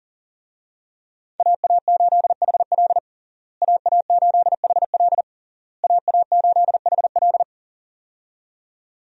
Morse code, heard as a keyed tone, about 40 wpm, 700 Hz. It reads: AA8HL AA8HL AA8HL
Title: USA Call Signs - 5 Characters - Repeated 3x - 40wpm